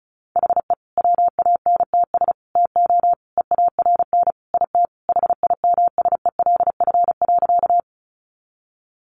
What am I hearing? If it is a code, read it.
Morse code, heard as a keyed tone, about 34 wpm, 700 Hz. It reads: HE WANTS TO EARN IT HIMSELF.
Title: Additional Easy Phrases 35wpm